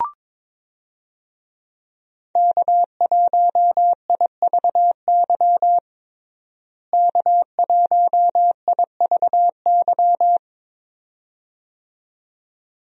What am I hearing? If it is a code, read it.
Morse code, heard as a keyed tone, about 22 wpm, 700 Hz. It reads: K1IVY K1IVY
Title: USA Call Signs - 5 Characters - Repeated 2x - 22wpm